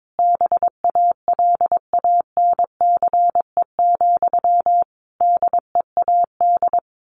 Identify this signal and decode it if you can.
Morse code, heard as a keyed tone, about 22 wpm, 700 Hz. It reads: BALANCE, DEAD
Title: A Sound Of Thunder 22wpm